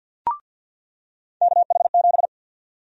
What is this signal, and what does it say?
Morse code, heard as a keyed tone, about 47 wpm, 700 Hz. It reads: KH7
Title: USA Call Sign Prefixes - 50wpm